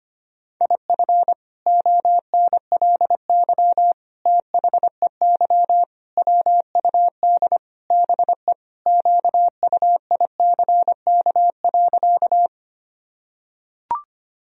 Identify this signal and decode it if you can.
Morse code, heard as a keyed tone, about 25 wpm, 700 Hz. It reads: IF ONLY THEY WUD BE QUICK.
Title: Sentences from Top 500 Words and Abbreviations 25wpm